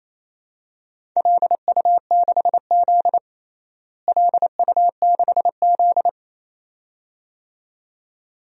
Morse code, heard as a keyed tone, about 28 wpm, 700 Hz. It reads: LU6Z LU6Z
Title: DXCC-100 Call Signs - 4 Characters - Repeated 2x - 28wpm